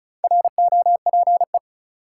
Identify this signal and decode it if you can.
Morse code, heard as a keyed tone, about 36 wpm, 700 Hz. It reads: ROPE
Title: Four Letter Words - 35wpm